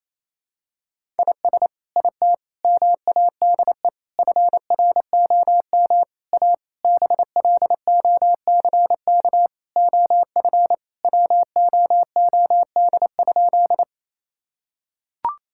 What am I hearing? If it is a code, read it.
Morse code, heard as a keyed tone, about 28 wpm, 700 Hz. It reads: IS IT MADE FROM A BLOCK OF WOOD?
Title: Sentences from Top 1600 Words 28wpm